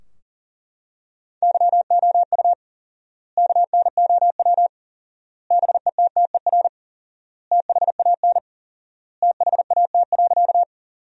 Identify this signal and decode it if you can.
Morse code, heard as a keyed tone, about 40 wpm, 700 Hz. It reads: YOU KNOW BETTER THAN THAT.
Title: 4x Word-Spacing - Sentences from Top 400 Words - 40wpm